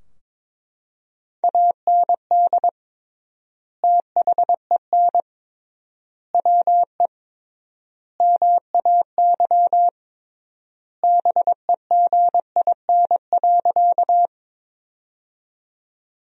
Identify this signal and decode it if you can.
Morse code, heard as a keyed tone, about 22 wpm, 700 Hz. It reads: AND THEN WE MAY BEGIN.
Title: 3x Word-Spacing - Sentences from Top 300 Words - 22wpm